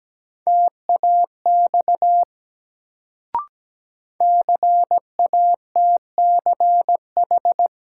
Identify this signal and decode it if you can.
Morse code, heard as a keyed tone, about 17 wpm, 700 Hz. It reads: TAX CATCH
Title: Sentences from Top 1500 Words - New Word Drill Alt - 17wpm